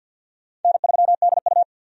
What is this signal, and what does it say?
Morse code, heard as a keyed tone, about 50 wpm, 700 Hz. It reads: N3DU
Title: USA Call Signs - 4 Characters - 50wpm